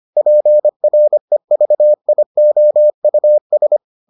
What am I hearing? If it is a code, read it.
Morse code, heard as a keyed tone, about 25 wpm, 600 Hz. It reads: PREVIOUS